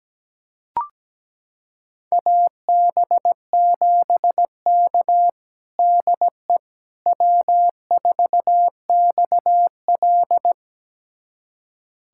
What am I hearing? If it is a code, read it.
Morse code, heard as a keyed tone, about 17 wpm, 700 Hz. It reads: AB7K DE W4XL
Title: USA Callsign DE Callsign - 4 Characters - 17wpm